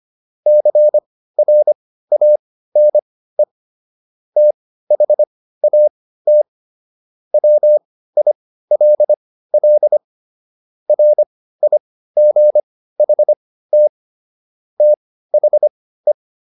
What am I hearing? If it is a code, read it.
Morse code, heard as a keyed tone, about 25 wpm, spaced as for 15 wpm, 600 Hz. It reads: CRANE THAT WILL RIGHT THE